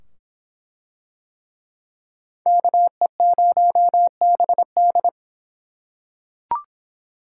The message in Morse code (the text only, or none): KE0BD